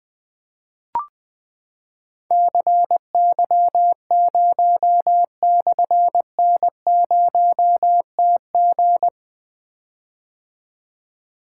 Morse code, heard as a keyed tone, about 20 wpm, 700 Hz. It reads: CY0/N0TG
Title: Difficult Call Signs 20wpm